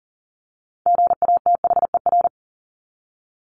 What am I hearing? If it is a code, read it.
Morse code, heard as a keyed tone, about 40 wpm, 700 Hz. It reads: GATHER
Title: Top 1501-1750 Words 40wpm